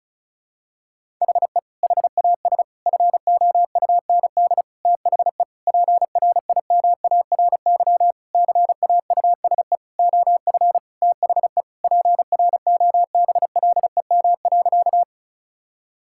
Morse code, heard as a keyed tone, about 34 wpm, 700 Hz. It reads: HE HAS FOUND THE PRIMARY CAUSE OF THE PROBLEM.
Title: Sentences from Top 1000 Words - Review Part 5 - 35wpm